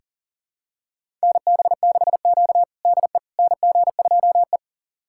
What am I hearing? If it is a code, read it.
Morse code, heard as a keyed tone, about 40 wpm, 700 Hz. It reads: NB6Q DE NG2E
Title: USA Callsign DE Callsign - 4 Characters - 40wpm